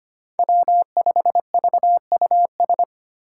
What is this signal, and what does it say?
Morse code, heard as a keyed tone, about 25 wpm, 700 Hz. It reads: W5VUS
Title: Five Challenging Letter-Number Combinations 25wpm